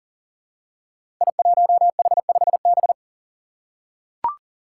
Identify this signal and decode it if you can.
Morse code, heard as a keyed tone, about 40 wpm, 700 Hz. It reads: I1H5B
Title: Five Challenging Letter-Number Combinations 40wpm